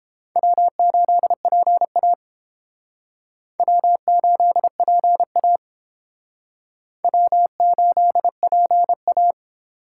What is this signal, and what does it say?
Morse code, heard as a keyed tone, about 29 wpm, 700 Hz. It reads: W8PA W8PA W8PA